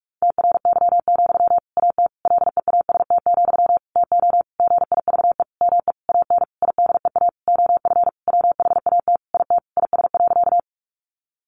Morse code, heard as a keyed tone, about 44 wpm, 700 Hz. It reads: TRY, AT LEAST, TO GIVE ME AN IDEA OF WHAT IT IS.
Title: Sentences from Top 700 Words 45wpm